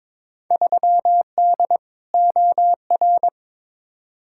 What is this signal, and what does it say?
Morse code, heard as a keyed tone, about 22 wpm, 700 Hz. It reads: 3D OR
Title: Field Day Exchanges Encore - 22wpm